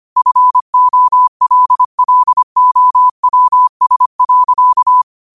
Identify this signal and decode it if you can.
Morse code, heard as a keyed tone, about 25 wpm, 1000 Hz. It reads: FOLLOWS.
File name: 25wpm